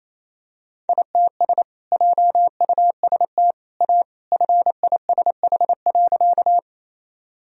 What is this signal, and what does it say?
Morse code, heard as a keyed tone, about 28 wpm, 700 Hz. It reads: ITS JUST A FISH.
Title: Sentences from Top 1000 Words - Review Part 1 - 28wpm